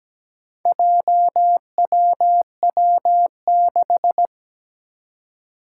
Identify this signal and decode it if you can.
Morse code, heard as a keyed tone, about 17 wpm, 700 Hz. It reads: JWW6
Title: Four Challenging Letter-Number Combinations 17wpm